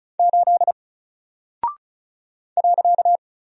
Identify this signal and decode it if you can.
Morse code, heard as a keyed tone, about 36 wpm, 700 Hz. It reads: 8 .
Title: Single Character (Letter:Number:Punctuation:Prosign) 35wpm